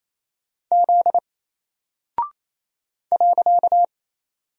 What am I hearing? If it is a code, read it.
Morse code, heard as a keyed tone, about 27 wpm, 700 Hz. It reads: Z .
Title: Single Character (Letter:Number:Punctuation:Prosign) Alt 28wpm